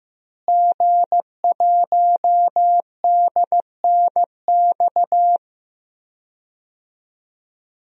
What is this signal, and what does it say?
Morse code, heard as a keyed tone, about 15 wpm, 700 Hz. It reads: G1DNX
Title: DXCC-100 Call Signs - 5 Characters - Repeated 3x - 15wpm